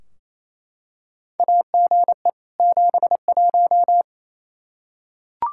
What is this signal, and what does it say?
Morse code, heard as a keyed tone, about 28 wpm, 700 Hz. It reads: AGE 71
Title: Practice QSO - Exchange 3 - More Info - 28wpm